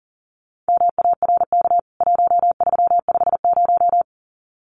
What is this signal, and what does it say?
Morse code, heard as a keyed tone, about 40 wpm, 700 Hz. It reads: MARK 1350
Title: CWT Exchange 40wpm